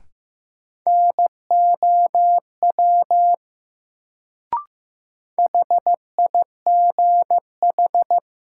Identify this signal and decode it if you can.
Morse code, heard as a keyed tone, about 15 wpm, 700 Hz. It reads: NOW HIGH
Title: Sets of 2 Words - New Word Drill - 15wpm